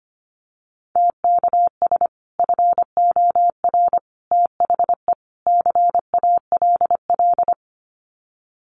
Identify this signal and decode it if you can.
Morse code, heard as a keyed tone, about 25 wpm, 700 Hz. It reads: TKS FOR THE CALL
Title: Common QSO Phrases 25wpm